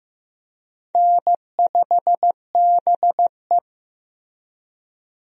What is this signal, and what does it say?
Morse code, heard as a keyed tone, about 16 wpm, 700 Hz. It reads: N5BE
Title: USA Call Signs - 4 Characters - Repeated 2x - 15wpm